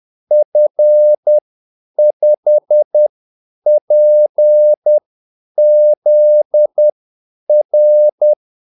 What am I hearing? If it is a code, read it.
Morse code, heard as a keyed tone, about 10 wpm, spaced as for 8 wpm, 600 Hz. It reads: F5PZR
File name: Callsigns-10-8wpm